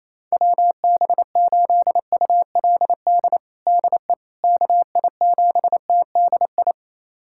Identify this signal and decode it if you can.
Morse code, heard as a keyed tone, about 28 wpm, 700 Hz. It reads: WB8ULD DE KI7TDI
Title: USA Callsign DE Callsign - 6 Characters - 28wpm